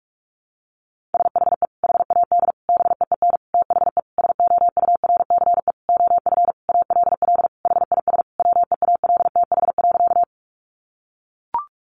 Morse code, heard as a keyed tone, about 44 wpm, 700 Hz. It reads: SHE HAD BEEN THE SOURCE OF ALL HIS WEALTH.
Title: Sentences from Top 1200 Words 45wpm